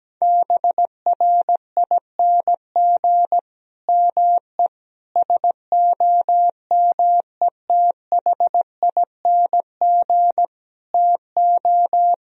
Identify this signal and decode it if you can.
Morse code, heard as a keyed tone, about 17 wpm, 700 Hz. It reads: BRING ME SOMETHING TO
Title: Sentences from Top 1000 Words - Review Part 3 - 17wpm